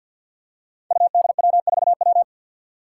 Morse code, heard as a keyed tone, about 47 wpm, 700 Hz. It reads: UDW4W
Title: Five Challenging Letter-Number Combinations 50wpm